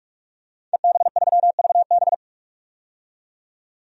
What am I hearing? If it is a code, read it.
Morse code, heard as a keyed tone, about 46 wpm, 700 Hz. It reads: EB3VB
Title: DXCC-100 Call Signs - 5 Characters - 45wpm